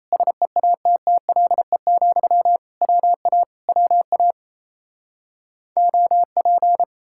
Morse code, heard as a keyed tone, about 38 wpm, 700 Hz. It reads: SEATTLE, WA WA OP